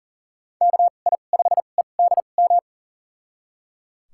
Morse code, heard as a keyed tone, about 38 wpm, spaced as for 28 wpm, 700 Hz. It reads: KI5EDM
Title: USA Call Signs - 2x Character Spacing - 40wpm